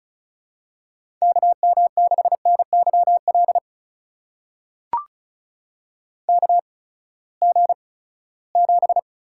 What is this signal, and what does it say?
Morse code, heard as a keyed tone, about 36 wpm, spaced as for 20 wpm, 700 Hz. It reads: KM6NYL KG7
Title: USA Call Signs - 6 Characters - 8x Character Spacing - 35wpm